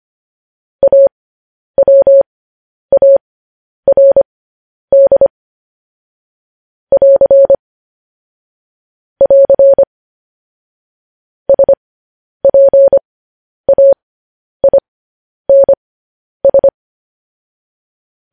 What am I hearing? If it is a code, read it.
Morse code, heard as a keyed tone, about 25 wpm, spaced as for 10 wpm, 550 Hz. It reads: AWARD <AR> <AR> SPAINS